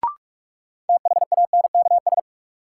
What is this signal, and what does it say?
Morse code, heard as a keyed tone, about 46 wpm, 700 Hz. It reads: THANKS
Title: Sentences from Top 600 Words - New Word Drill - 45wpm